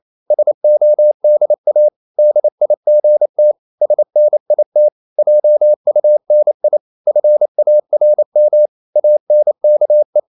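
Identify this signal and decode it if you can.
Morse code, heard as a keyed tone, about 28 wpm, 600 Hz. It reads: SODA DIGT SNIT JUNI FARM ANKE
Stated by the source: Danish short words sent in Morse at 28 wpm, from CW med Gnister.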